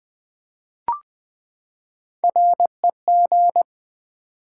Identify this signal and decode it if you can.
Morse code, heard as a keyed tone, about 20 wpm, 700 Hz. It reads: REG
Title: Build A Word - 3 Letters At A Time - 6-Letter Words - 20wpm